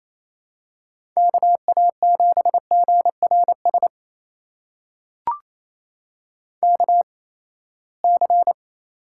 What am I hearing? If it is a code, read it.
Morse code, heard as a keyed tone, about 27 wpm, spaced as for 19 wpm, 700 Hz. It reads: KA7GRS KC
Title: USA Call Signs - 6 Characters - 8x Character Spacing - 28wpm